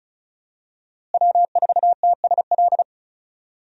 Morse code, heard as a keyed tone, about 34 wpm, 700 Hz. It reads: W4TSL
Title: USA Call Signs - 5 Characters - Repeated 2x - 35wpm